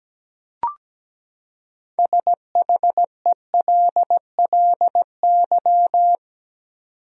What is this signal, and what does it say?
Morse code, heard as a keyed tone, about 17 wpm, 700 Hz. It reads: SHELLY
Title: USA Common Names - 17wpm